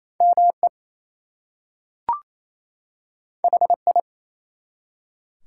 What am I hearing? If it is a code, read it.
Morse code, heard as a keyed tone, about 27 wpm, 700 Hz. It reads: ME HI
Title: Two Letter Words - 28wpm